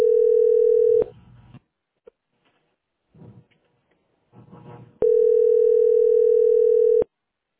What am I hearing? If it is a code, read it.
Ringback tone (North America), 440 + 480 Hz.